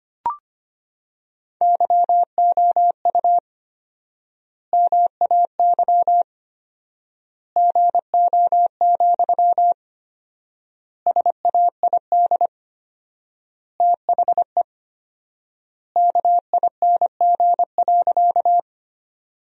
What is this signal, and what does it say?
Morse code, heard as a keyed tone, about 25 wpm, 700 Hz. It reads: YOU MAY GO, SAID THE KING.
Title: 4x Word-Spacing - Sentences from Top 400 Words - 25wpm